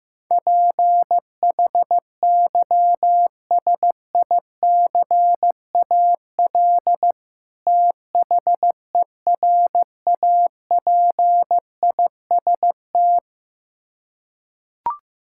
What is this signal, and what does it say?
Morse code, heard as a keyed tone, about 15 wpm, 700 Hz. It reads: PHYSICAL THERAPIST
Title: Practice QSO - Exchange 3 - More Info - 15wpm